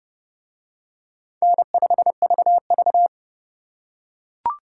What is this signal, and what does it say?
Morse code, heard as a keyed tone, about 27 wpm, 700 Hz. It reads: N5VV